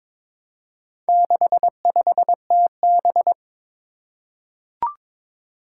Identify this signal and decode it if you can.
Morse code, heard as a keyed tone, about 22 wpm, 700 Hz. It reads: 65TB
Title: Four Challenging Letter-Number Combinations 22wpm